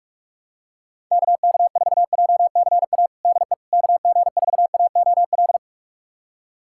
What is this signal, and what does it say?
Morse code, heard as a keyed tone, about 46 wpm, 700 Hz. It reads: KK4JCA DE KG4AOL
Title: USA Callsign DE Callsign - 6 Characters - 45wpm